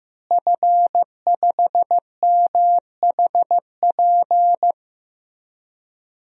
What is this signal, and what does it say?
Morse code, heard as a keyed tone, about 16 wpm, 700 Hz. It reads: F5MHP
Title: DXCC-100 Call Signs - 5 Characters - Repeated 2x - 15wpm